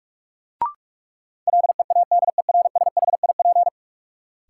Morse code, heard as a keyed tone, about 42 wpm, 700 Hz. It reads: LEADERSHIP
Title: Sentences from Top 1600 Words - New Word Drill - 45wpm